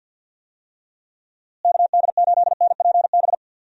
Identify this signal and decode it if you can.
Morse code, heard as a keyed tone, about 46 wpm, 700 Hz. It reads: KD8NPB
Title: USA Call Signs - 6 Characters - Repeated 3x - 50wpm